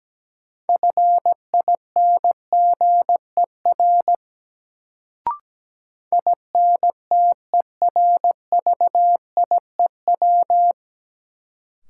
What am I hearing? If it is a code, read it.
Morse code, heard as a keyed tone, about 17 wpm, 700 Hz. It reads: FINGER INTERVIEW
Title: Sentences from Top 1600 Words - New Word Drill - 17wpm